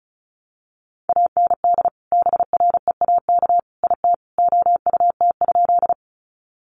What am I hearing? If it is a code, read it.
Morse code, heard as a keyed tone, about 34 wpm, 700 Hz. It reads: AND BREAK IT OUT?
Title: Sentences from Top 900 Words 35wpm